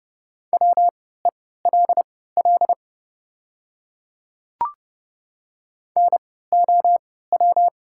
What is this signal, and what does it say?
Morse code, heard as a keyed tone, about 30 wpm, spaced as for 17 wpm, 700 Hz. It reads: WELL NOW
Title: 3x Character Spacing - Top 200 Words - 30wpm